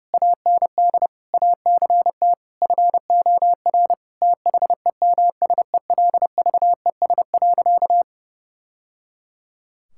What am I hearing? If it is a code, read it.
Morse code, heard as a keyed tone, about 30 wpm, 700 Hz. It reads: AND ACT FOR THEMSELVES.
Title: Sentences from Top 1000 Words - Review Part 5 - 30wpm